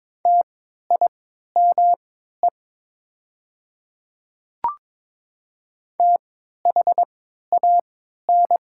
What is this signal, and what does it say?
Morse code, heard as a keyed tone, about 22 wpm, spaced as for 12 wpm, 700 Hz. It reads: TIME THAN
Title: 3x Character Spacing - Top 300 Words - 22wpm